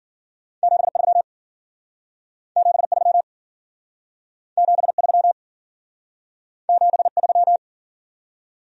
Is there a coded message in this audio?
Yes: Morse code from 0.6 to 7.6 s, heard as a keyed tone, about 52 wpm, 700 Hz. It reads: 73 73 73 73